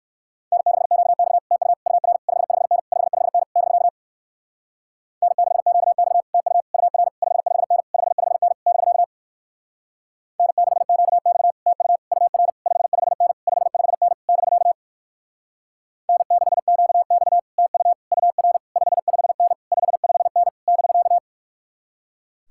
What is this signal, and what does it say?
Morse code, heard as a keyed tone, about 59 wpm, 700 Hz. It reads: N6QX TU UR 55N 55N <BK> N6QX TU UR 55N 55N <BK> N6QX TU UR 55N 55N <BK> N6QX TU UR 55N 55N <BK>